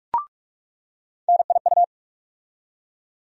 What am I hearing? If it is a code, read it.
Morse code, heard as a keyed tone, about 43 wpm, 700 Hz. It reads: NIU
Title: USA Call Sign Postfixes - 3 Letters 45wpm